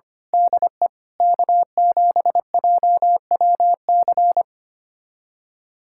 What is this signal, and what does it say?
Morse code, heard as a keyed tone, about 25 wpm, 700 Hz. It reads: DE K7JWC
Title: USA Callsign DE Callsign - 5 Characters - 25wpm